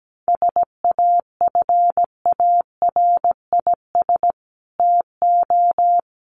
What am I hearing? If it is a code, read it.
Morse code, heard as a keyed tone, about 17 wpm, 700 Hz. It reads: SAFARIS TO
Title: A Sound Of Thunder 17wpm